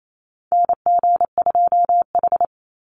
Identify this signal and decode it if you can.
Morse code, heard as a keyed tone, about 28 wpm, 700 Hz. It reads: NG2H
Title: Call Signs - 28wpm